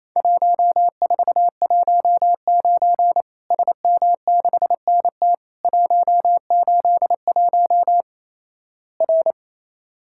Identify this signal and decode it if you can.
Morse code, heard as a keyed tone, about 28 wpm, 700 Hz. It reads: 1419 SM6NT 181 R
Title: Worked All Europe DX Contest - QTC Exchange - 28wpm